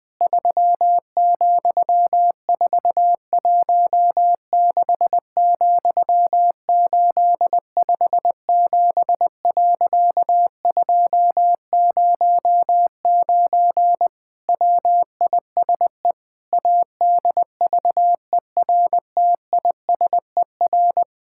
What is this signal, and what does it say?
Morse code, heard as a keyed tone, about 20 wpm, 700 Hz. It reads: 3,416,857.209 WISE ADVERTISER
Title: Pangram Challenge v2 - 20wpm